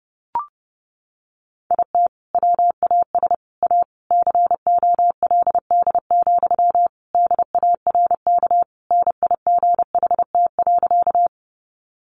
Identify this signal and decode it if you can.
Morse code, heard as a keyed tone, about 30 wpm, 700 Hz. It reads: IT WAS A COLD, DARK NIGHT.
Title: Sentences from Top 500 Words 30wpm